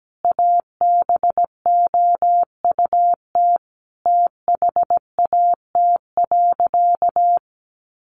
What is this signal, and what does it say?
Morse code, heard as a keyed tone, about 17 wpm, 700 Hz. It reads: ABOUT THAT.
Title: Sentences from Top 1500 Words 17wpm